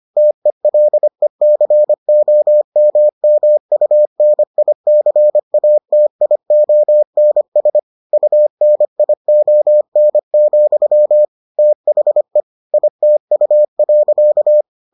Morse code, heard as a keyed tone, about 25 wpm, 600 Hz. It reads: TELECOMMUNICATIONS UNION, THE ITU.